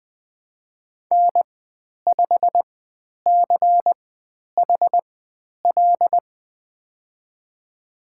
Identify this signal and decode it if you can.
Morse code, heard as a keyed tone, about 20 wpm, spaced as for 10 wpm, 700 Hz. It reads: N5CHL
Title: 095 Lesson 16g - Letter c - Call Signs 20wpm